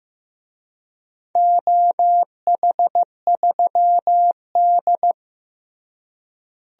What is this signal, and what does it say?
Morse code, heard as a keyed tone, about 16 wpm, 700 Hz. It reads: OH3D
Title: DXCC-100 Call Signs - 4 Characters - Repeated 3x - 15wpm